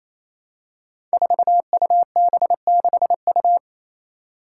Morse code, heard as a keyed tone, about 28 wpm, 700 Hz. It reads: 4UB6U
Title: Five Challenging Letter-Number Combinations 28wpm